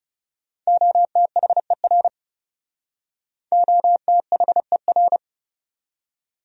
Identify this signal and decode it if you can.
Morse code, heard as a keyed tone, about 37 wpm, 700 Hz. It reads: OTHER OTHER
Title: Speed-Racing Top 100 Words 30wpm